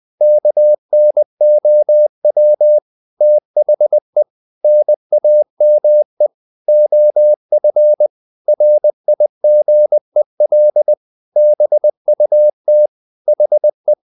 Morse code, heard as a keyed tone, about 20 wpm, 600 Hz. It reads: KNOW THE NAME OF RIGEL BUT HE
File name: the old man and the sea part 2 20wpm